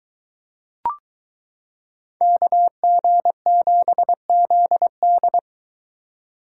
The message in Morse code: KG7ZD

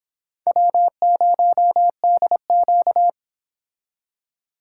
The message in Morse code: W0DQ

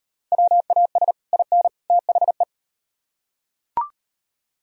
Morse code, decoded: WAS IN THE

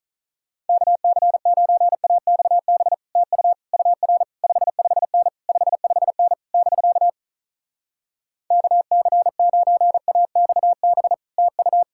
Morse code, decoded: KC9AXB TU UR 55N 55N <BK> KC9AXB TU

